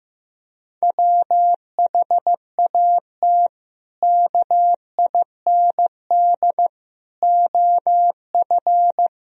Morse code, heard as a keyed tone, about 15 wpm, 700 Hz. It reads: WHAT KIND OF
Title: Sentences from Top 1000 Words - Review Part 4 - 15wpm